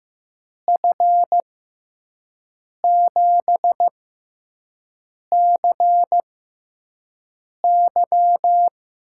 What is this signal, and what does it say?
Morse code, heard as a keyed tone, about 15 wpm, spaced as for 5 wpm, 700 Hz. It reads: F7CY
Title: USA Call Signs - 6 Characters - 6x Character Spacing - 15wpm